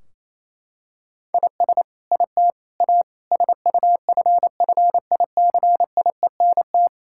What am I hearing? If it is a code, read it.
Morse code, heard as a keyed tone, about 28 wpm, 700 Hz. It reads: IS IT A SUFFICIENT
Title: Sentences from Top 1600 Words 28wpm